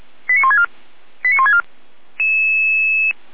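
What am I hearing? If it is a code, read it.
Five-tone selective call (ZVEI), heard as quick runs of five tones, tones of about 70 ms. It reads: 89164 89164
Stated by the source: ZVEI five-tone call 89376 received on SDR.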